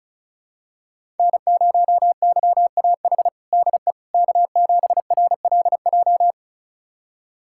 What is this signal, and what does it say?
Morse code, heard as a keyed tone, about 36 wpm, 700 Hz. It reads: N0YAH DE K7RLJ
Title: USA Callsign DE Callsign - 5 Characters - 35wpm